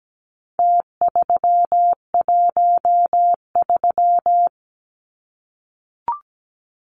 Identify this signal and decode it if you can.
Morse code, heard as a keyed tone, about 17 wpm, 700 Hz. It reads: T313
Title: Numbers - Contest Style - 4-Digit 17wpm